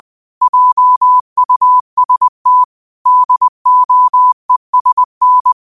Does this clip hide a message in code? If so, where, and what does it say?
Yes: Morse code from 0.4 to 5.5 s, heard as a keyed tone, about 20 wpm, 1000 Hz. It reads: JUST DOESN